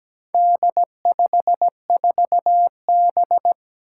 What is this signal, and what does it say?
Morse code, heard as a keyed tone, about 17 wpm, 700 Hz. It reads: D54B
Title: Four Challenging Letter-Number Combinations 17wpm